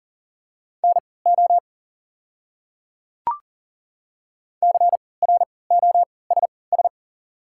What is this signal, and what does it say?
Morse code, heard as a keyed tone, about 40 wpm, spaced as for 23 wpm, 690 Hz. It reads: NO CROSS
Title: 3x Character Spacing - Top 300 Words - 40wpm